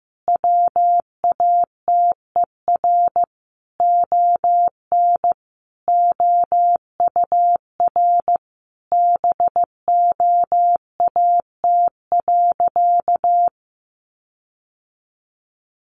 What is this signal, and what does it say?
Morse code, heard as a keyed tone, about 15 wpm, 700 Hz. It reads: WATER ON OUR BOAT.
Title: Sentences from Top 1400 Words 15wpm